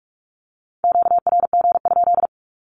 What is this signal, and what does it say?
Morse code, heard as a keyed tone, about 48 wpm, 700 Hz. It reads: QRG?
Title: Q-Codes 45wpm